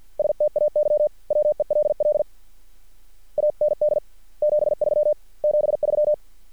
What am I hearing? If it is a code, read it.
Morse code, heard as a keyed tone, about 50 wpm, 600 Hz. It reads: STAY WELL AND 73 73
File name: archived / CRSnet